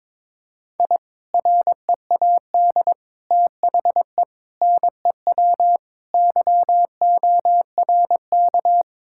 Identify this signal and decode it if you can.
Morse code, heard as a keyed tone, about 21 wpm, 700 Hz. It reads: I READ THE NEW YORK
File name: practice / 22wpm / Sentences with 3-Word Phrases - 22wpm